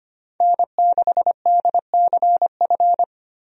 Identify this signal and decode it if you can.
Morse code, heard as a keyed tone, about 25 wpm, 700 Hz. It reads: N6DCF
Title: USA Call Signs - 5 Characters - 25wpm